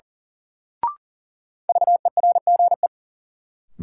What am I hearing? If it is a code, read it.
Morse code, heard as a keyed tone, about 37 wpm, 700 Hz. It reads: VERGE
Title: Sets of 4 Words - New Word Drill - 40wpm